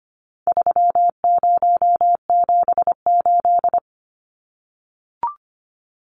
Morse code, heard as a keyed tone, about 25 wpm, 700 Hz. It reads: 3078
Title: Numbers - 4-Digit 25wpm